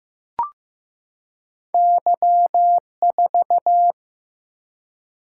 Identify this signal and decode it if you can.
Morse code, heard as a keyed tone, about 15 wpm, 700 Hz. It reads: Y4
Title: Single-Letter Single-Number 15wpm